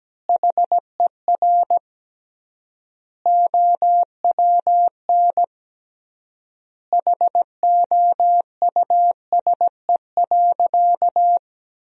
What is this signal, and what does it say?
Morse code, heard as a keyed tone, about 17 wpm, 700 Hz. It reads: HER OWN HOUSE.
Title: 3x Word-Spacing - Sentences from Top 300 Words - 17wpm